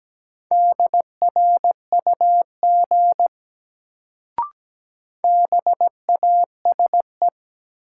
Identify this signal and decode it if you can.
Morse code, heard as a keyed tone, about 17 wpm, 700 Hz. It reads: DRUG BASE
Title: Sets of 5 Words - New Word Drill - 17wpm